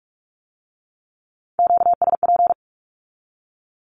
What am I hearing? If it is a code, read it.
Morse code, heard as a keyed tone, about 46 wpm, 700 Hz. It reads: QSP
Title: Q-Codes with Spoken Meaning - 45wpm